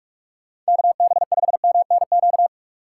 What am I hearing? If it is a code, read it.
Morse code, heard as a keyed tone, about 58 wpm, spaced as for 51 wpm, 700 Hz. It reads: KB5MNQ
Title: Speed Racing - USA 2x3 Call Signs 45wpm